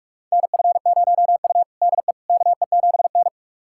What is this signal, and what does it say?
Morse code, heard as a keyed tone, about 42 wpm, 700 Hz. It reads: NF0U DE KE7N
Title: USA Callsign DE Callsign - 4 Characters - 45wpm